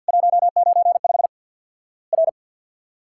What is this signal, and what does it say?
Morse code, heard as a keyed tone, about 50 wpm, 700 Hz. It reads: 195 R
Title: Worked All Europe DX Contest - QTC Exchange with Repeats - 50wpm